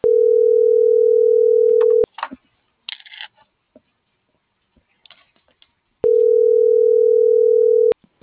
Ringback tone (North America), 440 + 480 Hz.